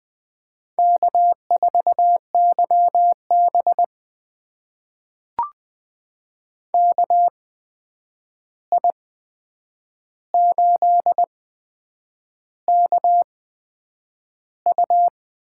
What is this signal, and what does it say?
Morse code, heard as a keyed tone, about 20 wpm, spaced as for 8 wpm, 700 Hz. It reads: K4YB KI8KU
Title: USA Call Signs - 8x Character Spacing - 20wpm